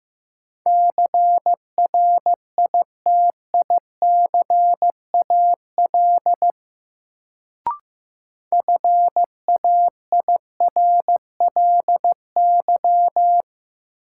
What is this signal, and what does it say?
Morse code, heard as a keyed tone, about 15 wpm, 700 Hz. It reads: CRITICAL FAIRLY
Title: Sentences from Top 1100 Words - New Word Drill Alt - 15wpm